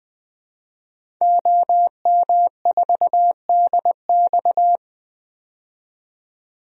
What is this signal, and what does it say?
Morse code, heard as a keyed tone, about 20 wpm, 700 Hz. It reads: OM4DX
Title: DXCC-100 Call Signs - 5 Characters - 20wpm